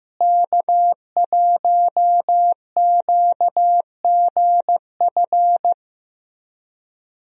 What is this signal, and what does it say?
Morse code, heard as a keyed tone, about 15 wpm, 700 Hz. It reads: K1QGF
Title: USA Call Signs - 5 Characters - Repeated 3x - 15wpm